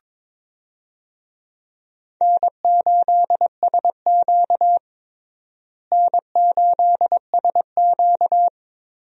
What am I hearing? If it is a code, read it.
Morse code, heard as a keyed tone, about 22 wpm, 700 Hz. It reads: N8SQ N8SQ
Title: USA Call Signs - 4 Characters - Repeated 2x - 22wpm